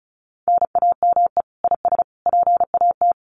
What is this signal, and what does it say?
Morse code, heard as a keyed tone, about 34 wpm, 700 Hz. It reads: NAME IS PAT
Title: Common QSO Phrases 35wpm